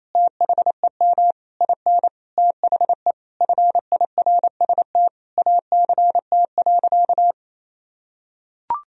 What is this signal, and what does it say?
Morse code, heard as a keyed tone, about 28 wpm, 700 Hz. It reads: THEM IN THE FIRST ACT.
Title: Sentences from Top 1000 Words - Review Part 2 - 28wpm